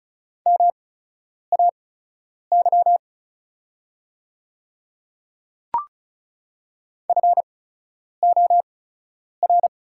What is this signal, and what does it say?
Morse code, heard as a keyed tone, about 36 wpm, spaced as for 10 wpm, 690 Hz. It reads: MAY FOR
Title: 8x Character Spacing - Top 200 Words - 35wpm